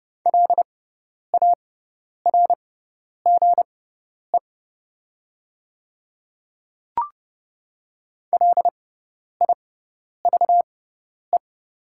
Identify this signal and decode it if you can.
Morse code, heard as a keyed tone, about 30 wpm, spaced as for 10 wpm, 700 Hz. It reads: LARGE LIVE
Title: 6x Character Spacing - Top 200 Words - 30wpm